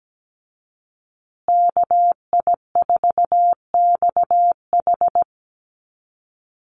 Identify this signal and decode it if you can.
Morse code, heard as a keyed tone, about 17 wpm, 700 Hz. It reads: KI4XH
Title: Call Signs Repeated 3x - 17wpm